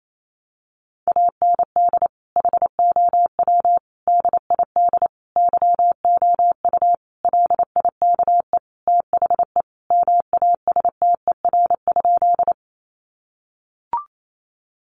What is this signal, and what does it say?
Morse code, heard as a keyed tone, about 28 wpm, 700 Hz. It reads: AND HOW DID YOU LIKE THE MASTER?
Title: Sentences from Top 1300 Words 28wpm